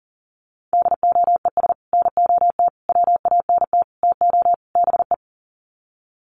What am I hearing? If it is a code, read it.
Morse code, heard as a keyed tone, about 40 wpm, 700 Hz. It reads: DOES NOT WANT TO BE
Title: Sets of 5 Words Encore 40wpm